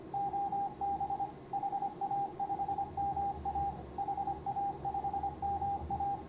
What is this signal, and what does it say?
Morse code, heard as a keyed tone, about 24 wpm, 810 Hz, with radio noise. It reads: OB HA5MA HA5MA
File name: CW / HA5MA RHOB HANG CW 2017